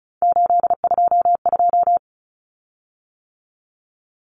Morse code, heard as a keyed tone, about 34 wpm, 700 Hz. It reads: 822
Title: Common Cut Numbers - 3-Digit 35wpm